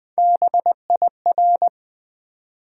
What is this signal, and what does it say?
Morse code, heard as a keyed tone, about 20 wpm, 700 Hz. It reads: BIR
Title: Build A Word - 1 Letter At A Time - 3 to 5-Letter Words - 20wpm